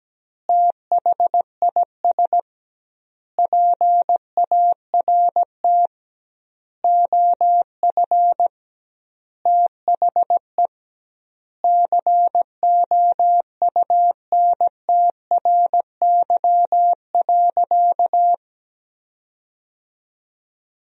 Morse code, heard as a keyed tone, about 17 wpm, 700 Hz. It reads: THIS PART OF THE COUNTRY.
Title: 2x Word-Spacing - Sentences from Top 300 Words - 17wpm